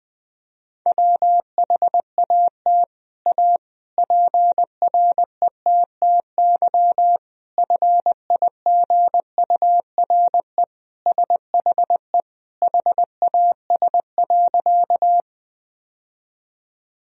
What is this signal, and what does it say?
Morse code, heard as a keyed tone, about 19 wpm, 700 Hz. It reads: WHAT A PRETTY FIGURE SHE HAS.
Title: Sentences from Top 1000 Words - Review Part 2 - 20wpm